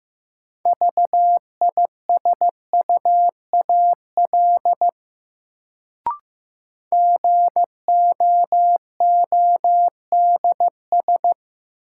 Morse code, heard as a keyed tone, about 15 wpm, 700 Hz. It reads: VISUAL GOODS
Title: Sentences from Top 1000 Words - Review Part 6 - New Word Drill - 15wpm